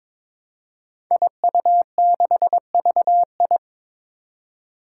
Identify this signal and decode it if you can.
Morse code, heard as a keyed tone, about 22 wpm, 700 Hz. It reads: IU6VI
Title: Five Challenging Letter-Number Combinations 22wpm